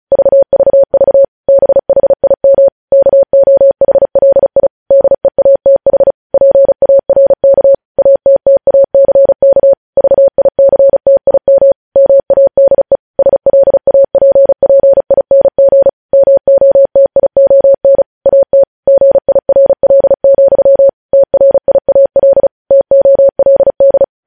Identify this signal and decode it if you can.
Morse code, heard as a keyed tone, about 36 wpm, 550 Hz. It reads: VVV BHIM KOHLI DEATH PARK ATTACK VICTIM MADE SLAPPING MOTION AT GIRL, TRIAL TOLD
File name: podcast / News headlines-1x-35x35